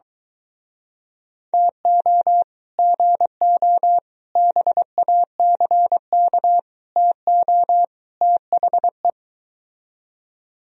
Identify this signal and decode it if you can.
Morse code, heard as a keyed tone, about 23 wpm, 700 Hz. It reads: TO GO BACK TO THE